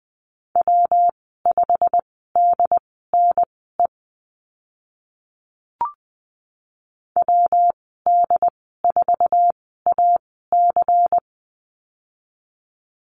Morse code, heard as a keyed tone, about 20 wpm, spaced as for 14 wpm, 700 Hz. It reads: W5DNE WD4AC
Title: 154 Lesson 25g - Letter p - Call Signs 20wpm